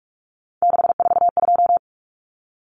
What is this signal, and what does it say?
Morse code, heard as a keyed tone, about 44 wpm, 700 Hz. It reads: 642
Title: Numbers - 3-Digit 45wpm